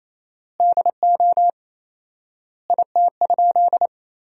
Morse code, heard as a keyed tone, about 27 wpm, 700 Hz. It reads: DO IT?
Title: 4x Word-Spacing - Sentences from Top 100 Words - 28wpm